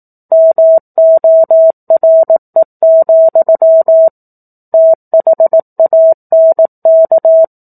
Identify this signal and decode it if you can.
Morse code, heard as a keyed tone, about 18 wpm, 650 Hz. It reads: MORE, THANK
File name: 04-Chapter-04 18 WPM